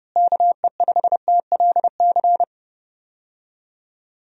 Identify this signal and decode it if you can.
Morse code, heard as a keyed tone, about 29 wpm, 700 Hz. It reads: KE5TLC
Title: USA Call Signs - 6 Characters - Repeated 3x - 30wpm